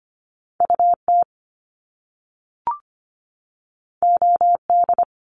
Morse code, heard as a keyed tone, about 25 wpm, 700 Hz. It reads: U T O D
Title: Top Two Letter Combinations in Words 25wpm